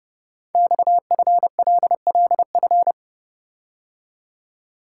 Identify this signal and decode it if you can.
Morse code, heard as a keyed tone, about 30 wpm, 700 Hz. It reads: XFLLF
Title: Code Groups - 5 Letters - 30wpm